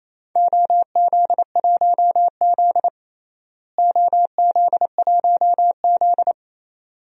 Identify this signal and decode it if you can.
Morse code, heard as a keyed tone, about 28 wpm, 700 Hz. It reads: OZ1Z OZ1Z
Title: DXCC-100 Call Signs - 4 Characters - Repeated 2x - 28wpm